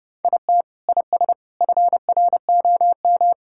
Morse code, heard as a keyed tone, about 32 wpm, 700 Hz. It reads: IT IS FROM
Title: Sentences from Top 1000 Words - Review Part 5 - 30wpm